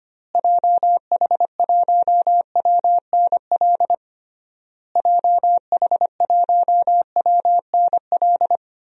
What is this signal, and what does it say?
Morse code, heard as a keyed tone, about 25 wpm, 700 Hz. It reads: JH1WNL JH1WNL
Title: DXCC-100 Call Signs - 6 Characters - Repeated 3x - 25wpm